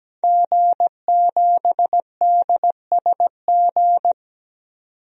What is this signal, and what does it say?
Morse code, heard as a keyed tone, about 17 wpm, 700 Hz. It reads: G7DSG
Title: DXCC-100 Call Signs - 5 Characters - 17wpm